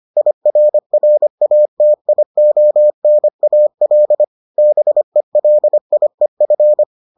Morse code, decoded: IRRATIONAL BELIEF

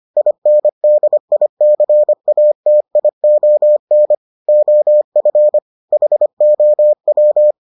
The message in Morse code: INDICATION OF HOW

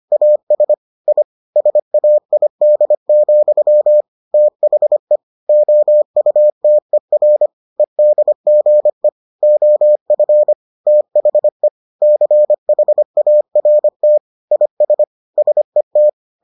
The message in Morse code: AS I SAID, THE OUTER EDGE OF THE CHART IS SET

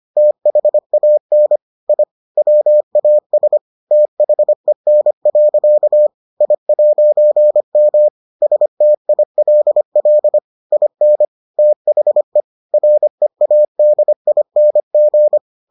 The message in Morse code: THAN I WAS THEN. I'M STILL IN THE READING

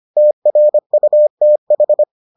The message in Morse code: TRUTH

